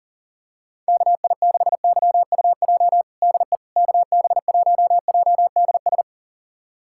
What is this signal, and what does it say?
Morse code, heard as a keyed tone, about 40 wpm, 700 Hz. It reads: KI6YUJ DE KB1JDS
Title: USA Callsign DE Callsign 40wpm